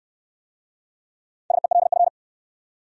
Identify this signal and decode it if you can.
Morse code, heard as a keyed tone, about 68 wpm, 700 Hz. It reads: SELF